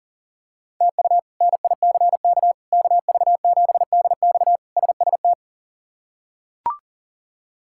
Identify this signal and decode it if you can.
Morse code, heard as a keyed tone, about 40 wpm, 700 Hz. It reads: TU NICK KV7DX SST
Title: K1USN SST QSO Practice - 40wpm